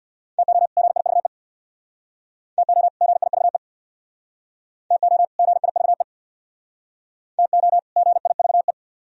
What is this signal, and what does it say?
Morse code, heard as a keyed tone, about 64 wpm, 700 Hz. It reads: TO GIVE TO GIVE TO GIVE TO GIVE